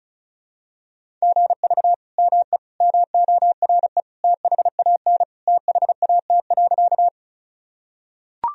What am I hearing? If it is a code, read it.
Morse code, heard as a keyed tone, about 34 wpm, 690 Hz. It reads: GV ME MORE THAN THAT.
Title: Sentences from Top 200 Words and Abbreviations 35wpm